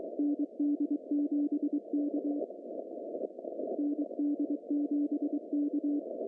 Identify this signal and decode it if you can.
Morse code, heard as a keyed tone, about 23 wpm, 290 Hz. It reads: ND7K ND7K
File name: CW / cqww-cw-k5tr